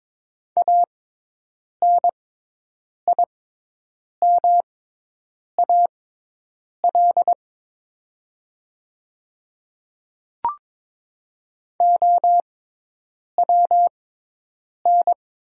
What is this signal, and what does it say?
Morse code, heard as a keyed tone, about 22 wpm, spaced as for 8 wpm, 700 Hz. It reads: ANIMAL OWN
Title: 6x Character Spacing - Top 300 Words - 22wpm